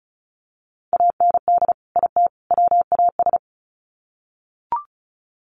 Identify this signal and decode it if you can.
Morse code, heard as a keyed tone, about 34 wpm, 700 Hz. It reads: AND IT WAS
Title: Sets of 3 Words 35wpm